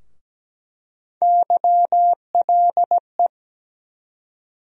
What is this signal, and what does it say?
Morse code, heard as a keyed tone, about 17 wpm, 700 Hz. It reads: YLE
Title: USA Call Sign Postfixes - 3 Letters 17wpm